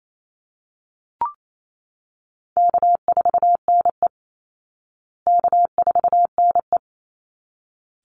Morse code, heard as a keyed tone, about 28 wpm, 700 Hz. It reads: K4NE K4NE
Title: Easy Call Signs Repeated 2x - 28wpm